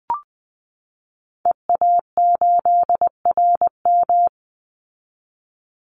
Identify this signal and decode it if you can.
Morse code, heard as a keyed tone, about 20 wpm, 700 Hz. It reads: EA8RM
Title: Call Signs Repeated 2x - 20wpm